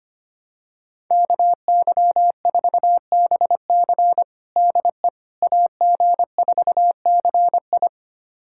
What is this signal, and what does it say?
Morse code, heard as a keyed tone, about 25 wpm, 700 Hz. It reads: KY4BC DE AG4CI
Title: USA Callsign DE Callsign - 5 Characters - 25wpm